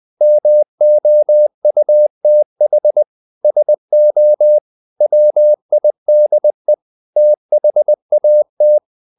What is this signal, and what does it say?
Morse code, heard as a keyed tone, about 20 wpm, 600 Hz. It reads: MOUTH SO WIDE THAT
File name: the old man and the sea part 2 20wpm